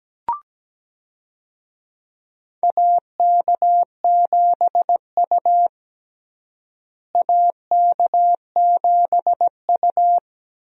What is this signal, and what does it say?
Morse code, heard as a keyed tone, about 17 wpm, 700 Hz. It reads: AK7U AK7U
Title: USA Call Signs - 4 Characters - Repeated 2x - 17wpm